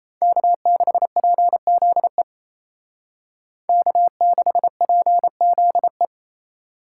Morse code, heard as a keyed tone, about 36 wpm, 700 Hz. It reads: K6PZE K6PZE